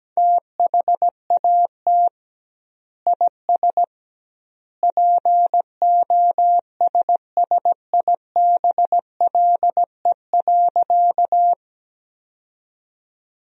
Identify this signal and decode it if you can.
Morse code, heard as a keyed tone, about 17 wpm, 700 Hz. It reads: THAT IS POSSIBLE.
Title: 2x Word-Spacing - Sentences from Top 500 Words - 17wpm